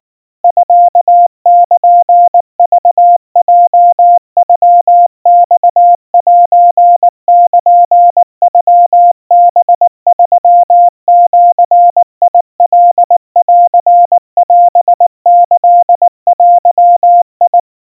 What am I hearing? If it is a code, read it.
Morse code (Japanese Wabun code), heard as a keyed tone, about 19 wpm, 700 Hz. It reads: ミルクヲノマセルノハ3ジカンオキデ
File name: kawa / Morse sample / kamyu 70 cpm 700 Hz Crispy